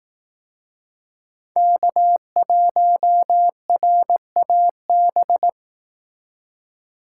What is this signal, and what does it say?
Morse code, heard as a keyed tone, about 18 wpm, 700 Hz. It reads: K1RAB
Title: Speed Racing - USA 1x3 Call Signs 15wpm